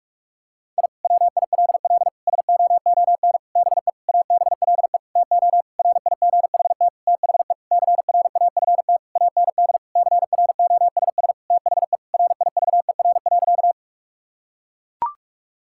Morse code, heard as a keyed tone, about 41 wpm, 700 Hz. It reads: I WILL SOON BE ABLE TO RIGHT THE CRAFT AND CROSS THE RIVER.
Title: Sentences from Top 1000 Words - Review Part 6 - 45wpm